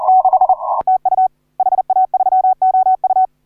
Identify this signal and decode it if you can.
Morse code, heard as a keyed tone, about 40 wpm, 750 Hz. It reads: 6 TU HA3OU
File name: AUDIO2019 / HA3OU CW